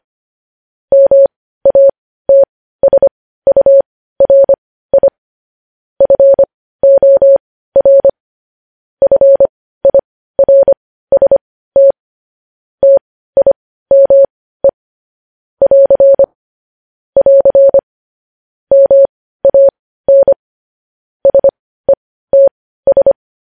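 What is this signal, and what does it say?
Morse code, heard as a keyed tone, about 25 wpm, spaced as for 15 wpm, 550 Hz. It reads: MATSURI FOR FIRST TIME <AR> <AR> MAN SETS
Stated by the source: Morse code news headlines sent at 15.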